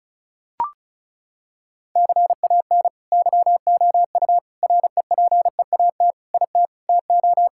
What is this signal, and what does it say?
Morse code, heard as a keyed tone, about 36 wpm, 700 Hz. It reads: CAN YOU REPEAT IT TO
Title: Sentences from Top 1600 Words 35wpm